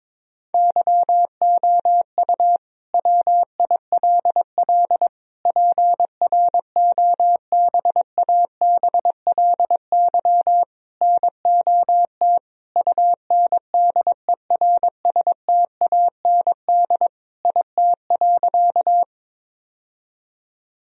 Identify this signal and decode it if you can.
Morse code, heard as a keyed tone, about 22 wpm, 700 Hz. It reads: YOU WILL PROBABLY NOT UNDERSTAND IT.
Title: Sentences from Top 1000 Words - Review Part 4 - 22wpm